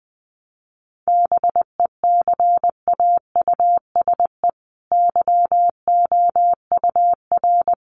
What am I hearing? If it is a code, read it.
Morse code, heard as a keyed tone, about 20 wpm, 700 Hz. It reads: BECAUSE YOUR
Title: Sentences from Top 1200 Words 20wpm